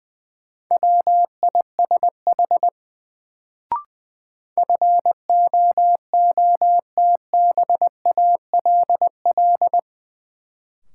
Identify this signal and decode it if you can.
Morse code, heard as a keyed tone, about 21 wpm, 700 Hz. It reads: WISH FOOTBALL
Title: Sentences from Top 1600 Words - New Word Drill - 20wpm